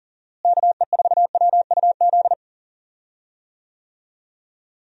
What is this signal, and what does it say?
Morse code, heard as a keyed tone, about 40 wpm, 700 Hz. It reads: KE4WUZ
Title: USA Call Signs - 6 Characters - Repeated 3x - 40wpm